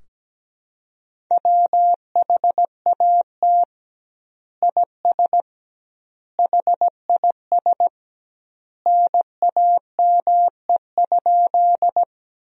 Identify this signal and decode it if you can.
Morse code, heard as a keyed tone, about 17 wpm, 700 Hz. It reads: WHAT IS HIS NAME?
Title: 2x Word-Spacing - Sentences from Top 200 Words - 17wpm